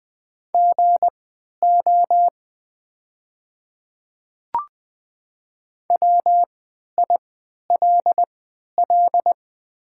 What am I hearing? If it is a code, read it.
Morse code, heard as a keyed tone, about 20 wpm, spaced as for 11 wpm, 700 Hz. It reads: GO WILL
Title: 3x Character Spacing - Top 200 Words - 20wpm